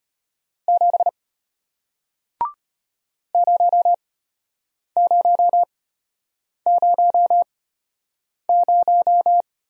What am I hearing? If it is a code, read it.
Morse code, heard as a keyed tone, about 34 wpm, spaced as for 8 wpm, 700 Hz. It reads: 7 0000